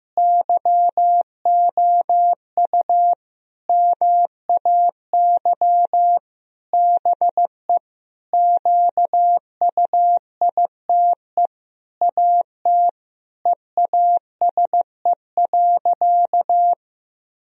Morse code, heard as a keyed tone, about 15 wpm, 700 Hz. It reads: YOU MAY BE QUITE AT EASE.
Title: Sentences from Top 1000 Words - Review Part 1 - 15wpm